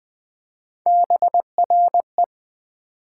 Morse code, heard as a keyed tone, about 20 wpm, 700 Hz. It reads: BRE
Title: Build A Word - 3 Letters At A Time - 6-Letter Words - 20wpm